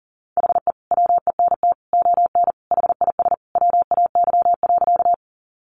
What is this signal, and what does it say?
Morse code, heard as a keyed tone, about 40 wpm, 700 Hz. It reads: HE WENT ON HIS WAY.
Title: Sentences from Top 200 Words 40wpm